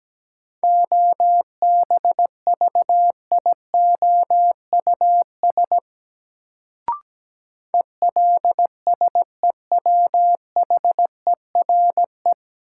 Morse code, heard as a keyed tone, about 17 wpm, 700 Hz. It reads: OBVIOUS ELSEWHERE
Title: Sentences from Top 1600 Words - New Word Drill - 17wpm